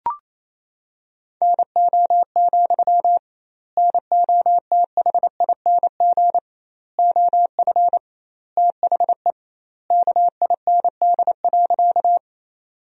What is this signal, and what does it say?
Morse code, heard as a keyed tone, about 28 wpm, 700 Hz. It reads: NO, NOTHING OF THE KIND.
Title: 2x Word-Spacing - Sentences from Top 500 Words - 28wpm